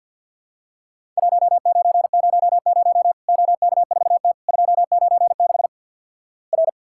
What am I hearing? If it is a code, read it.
Morse code, heard as a keyed tone, about 50 wpm, 700 Hz. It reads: 1900 OK4T 296 R
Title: Worked All Europe DX Contest - QTC Exchange - 50wpm